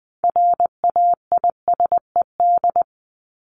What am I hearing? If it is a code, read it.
Morse code, heard as a keyed tone, about 20 wpm, 700 Hz. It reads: RAISED
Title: ED Suffix - Top 100 Words with Base Words 20wpm